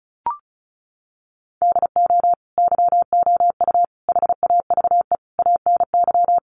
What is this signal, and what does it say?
Morse code, heard as a keyed tone, about 34 wpm, 700 Hz. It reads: DO YOU HAVE ANY
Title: Sentences from Top 1100 Words 35wpm